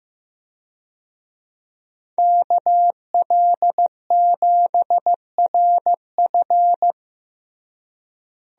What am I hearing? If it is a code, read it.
Morse code, heard as a keyed tone, about 15 wpm, 700 Hz. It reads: KL7RF
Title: DXCC-100 Call Signs - 5 Characters - Repeated 2x - 15wpm